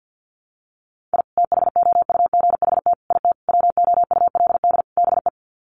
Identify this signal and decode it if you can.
Morse code, heard as a keyed tone, about 52 wpm, 700 Hz. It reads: I THOUGHT IT WOULD BE
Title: Sets of 5 Words Encore 50wpm